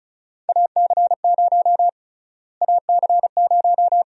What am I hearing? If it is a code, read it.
Morse code, heard as a keyed tone, about 36 wpm, 700 Hz. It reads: AC0 AC0
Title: USA Call Sign Prefixes - Repeated 2x - 35wpm